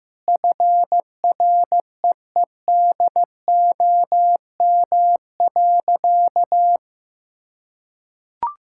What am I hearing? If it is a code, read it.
Morse code, heard as a keyed tone, about 15 wpm, 700 Hz. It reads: FREEDOM.
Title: Sentences from Top 1600 Words 15wpm